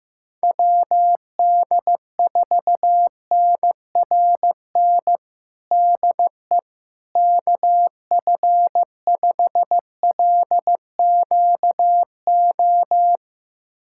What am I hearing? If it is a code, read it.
Morse code, heard as a keyed tone, about 15 wpm, 700 Hz. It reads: WD4NRN DE KF5LQO
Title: USA Callsign DE Callsign - 6 Characters - 15wpm